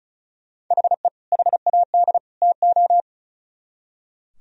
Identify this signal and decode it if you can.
Morse code, heard as a keyed tone, about 45 wpm, 700 Hz. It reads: HE HAD TO